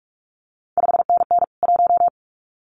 Morse code, heard as a keyed tone, about 46 wpm, 700 Hz. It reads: 5NN 1
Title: CQ WW DX Contest Exchange Encore - 45wpm